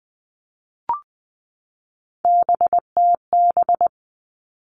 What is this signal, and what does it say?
Morse code, heard as a keyed tone, about 20 wpm, 700 Hz. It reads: BTB
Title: All Cut Numbers - 3-Digit 20wpm